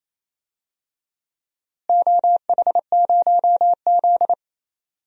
Morse code, heard as a keyed tone, about 28 wpm, 700 Hz. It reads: OH0Z
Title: DXCC-100 Call Signs - 4 Characters - Repeated 2x - 28wpm